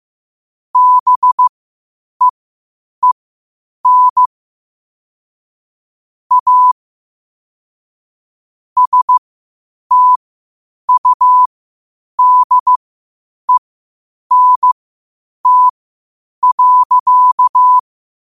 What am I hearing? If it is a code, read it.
Morse code, heard as a keyed tone, about 15 wpm, spaced as for 8 wpm, 1000 Hz. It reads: BEEN A STUDENT.
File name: qotdfiles / 8wpm